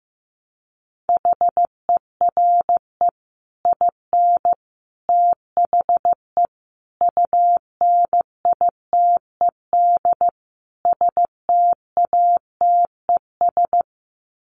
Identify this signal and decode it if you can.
Morse code, heard as a keyed tone, about 15 wpm, 700 Hz. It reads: HERE IN THE UNITED STATES
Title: Sets of 5 Words 15wpm